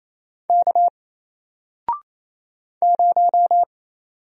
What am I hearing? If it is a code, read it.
Morse code, heard as a keyed tone, about 27 wpm, 700 Hz. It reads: K 0
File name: Single-Letter-Number-28wpm